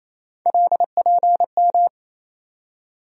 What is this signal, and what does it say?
Morse code, heard as a keyed tone, about 28 wpm, 700 Hz. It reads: LPM
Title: USA Call Sign Postfixes - 3 Letters 28wpm